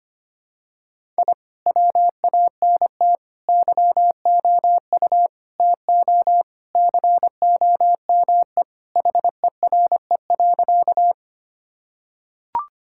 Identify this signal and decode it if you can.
Morse code, heard as a keyed tone, about 25 wpm, 700 Hz. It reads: I WANT YOU TO COME HERE.
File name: practice / 25wpm / Sentences with 2-Word Phrases - 25wpm